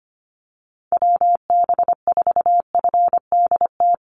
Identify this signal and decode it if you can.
Morse code, heard as a keyed tone, about 25 wpm, 700 Hz. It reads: WB4FDT
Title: Call Signs Repeated 3x - 25wpm